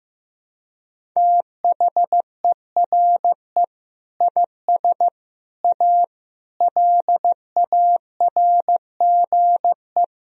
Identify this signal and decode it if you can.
Morse code, heard as a keyed tone, about 15 wpm, 700 Hz. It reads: THERE IS A LARGE
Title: Sentences from Top 1000 Words - Review Part 4 - 15wpm